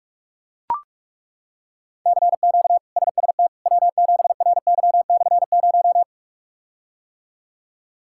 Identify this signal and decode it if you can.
Morse code, heard as a keyed tone, about 41 wpm, 700 Hz. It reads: CQ SST W7RY/0
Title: K1USN SST QSO Practice - 45wpm